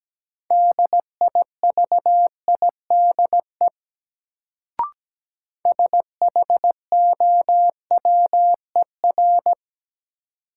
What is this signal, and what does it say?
Morse code, heard as a keyed tone, about 17 wpm, 700 Hz. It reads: DIVIDE SHOWER
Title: Sentences from Top 1600 Words - New Word Drill Alt - 17wpm